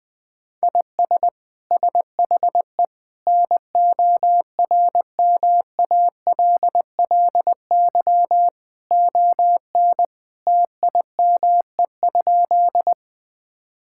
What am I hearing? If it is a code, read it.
Morse code, heard as a keyed tone, about 21 wpm, 700 Hz. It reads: IS SHE NORMALLY ON TIME?
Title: Sentences from Top 1000 Words - Review Part 6 - 20wpm